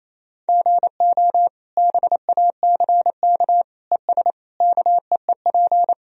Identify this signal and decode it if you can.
Morse code, heard as a keyed tone, about 27 wpm, 700 Hz. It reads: GO BACK ES KEEP
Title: Sentences from Top 300 Words and Abbreviations 28wpm